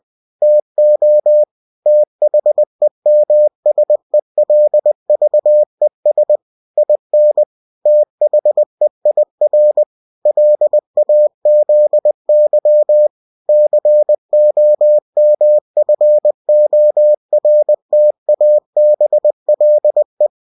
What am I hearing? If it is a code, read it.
Morse code, heard as a keyed tone, about 20 wpm, 600 Hz. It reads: TO THEMSELVES IN THEIR LAZY COMFORTABLE
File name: cw-books / winnie-the-pooh / winnie-20wpm-page-0102